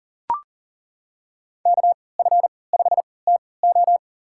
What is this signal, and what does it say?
Morse code, heard as a keyed tone, about 40 wpm, spaced as for 23 wpm, 700 Hz. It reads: KF5TO
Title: USA Call Signs - 5 Characters - 3x Character Spacing - 40wpm